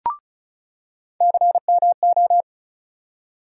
Morse code, heard as a keyed tone, about 34 wpm, 700 Hz. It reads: CMO
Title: USA Call Sign Postfixes - 3 Letters 35wpm